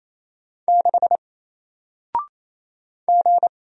Morse code, heard as a keyed tone, about 27 wpm, 700 Hz. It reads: 6 G
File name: Single-Letter-Number-28wpm